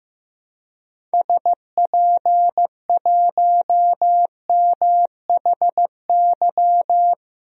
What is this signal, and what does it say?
Morse code, heard as a keyed tone, about 16 wpm, 700 Hz. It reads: SP1MHY
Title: DXCC-100 Call Signs - 6 Characters - Repeated 3x - 15wpm